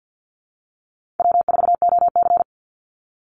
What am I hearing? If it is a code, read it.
Morse code, heard as a keyed tone, about 50 wpm, 700 Hz. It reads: W4OC
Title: Easy Call Signs - 50wpm